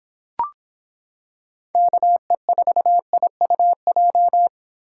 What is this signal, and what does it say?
Morse code, heard as a keyed tone, about 25 wpm, 700 Hz. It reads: KE4IUJ